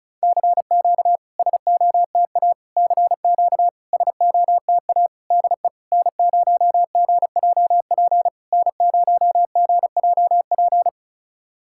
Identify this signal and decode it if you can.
Morse code, heard as a keyed tone, about 36 wpm, 700 Hz. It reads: CQ SOTA CQ SOTA DE N0GJP N0GJP
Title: SOTA QSO Practice 35wpm